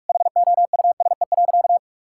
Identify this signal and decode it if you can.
Morse code, heard as a keyed tone, about 43 wpm, 700 Hz. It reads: HOUSE.
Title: Sentences from Top 1600 Words 45wpm